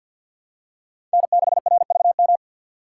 Morse code, heard as a keyed tone, about 48 wpm, 700 Hz. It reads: N6RVM
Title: USA Call Signs - 6x Character Spacing - 50wpm